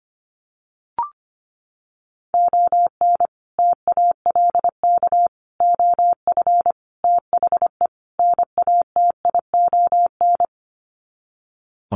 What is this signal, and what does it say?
Morse code, heard as a keyed tone, about 25 wpm, 700 Hz. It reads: ON TALK OF THE NATION
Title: Sets of 5 Words 25wpm